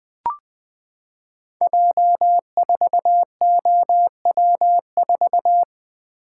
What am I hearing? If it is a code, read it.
Morse code, heard as a keyed tone, about 20 wpm, 700 Hz. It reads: J4OW4
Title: Code Groups - 5 Letter:Numbers - 20wpm